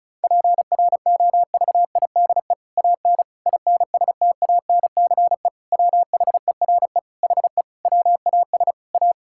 Morse code, heard as a keyed tone, about 36 wpm, 700 Hz. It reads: PROVIDE AN INSTANCE WHERE HE WAS A
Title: Sentences from Top 1000 Words - Review Part 5 - 35wpm